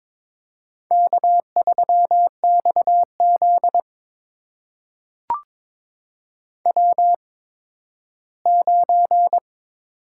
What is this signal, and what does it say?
Morse code, heard as a keyed tone, about 22 wpm, spaced as for 13 wpm, 700 Hz. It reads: K3XZ W9
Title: USA Call Signs - 4 Characters - 8x Character Spacing - 22wpm